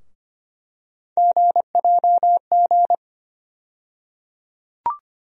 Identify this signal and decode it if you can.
Morse code, heard as a keyed tone, about 25 wpm, 700 Hz. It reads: GJG
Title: USA Call Sign Postfixes - 3 Letters 25wpm